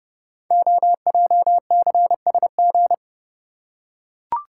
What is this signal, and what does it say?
Morse code, heard as a keyed tone, about 30 wpm, 690 Hz. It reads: OJCSG
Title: Code Groups - 5 Letters - 30wpm